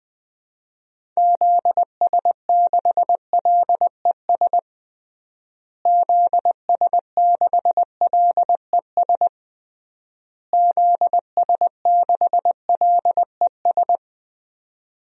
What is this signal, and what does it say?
Morse code, heard as a keyed tone, about 20 wpm, 700 Hz. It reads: ZS6LES ZS6LES ZS6LES
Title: DXCC-100 Call Signs - 6 Characters - Repeated 3x - 20wpm